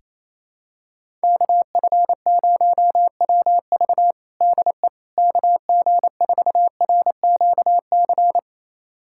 Morse code, heard as a keyed tone, about 28 wpm, 700 Hz. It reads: KF0WV DE KG4RQC
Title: USA Callsign DE Callsign 28wpm